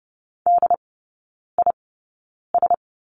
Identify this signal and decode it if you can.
Morse code, heard as a keyed tone, about 30 wpm, spaced as for 9 wpm, 700 Hz. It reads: DIS
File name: 081 Lesson 14e - Number 2 - Three Characters 30wpm